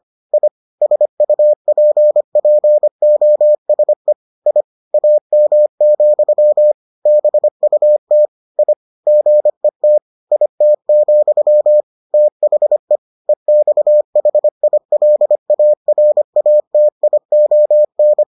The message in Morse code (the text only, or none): I SUPPOSE I AM, BUT I GET IT, THE EXHILARATION